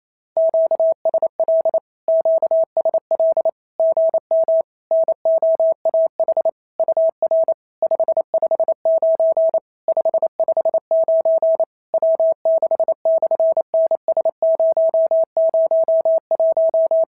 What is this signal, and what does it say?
Morse code, heard as a keyed tone, about 28 wpm, 650 Hz. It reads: QSL QSL GM NOAH UR 559 559 W6/NS001
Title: SOTA QSO Practice 28wpm